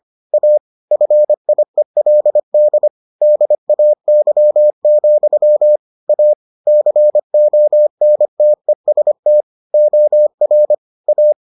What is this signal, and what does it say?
Morse code, heard as a keyed tone, about 25 wpm, 600 Hz. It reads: A FIELD DAY, A CONTEST OR A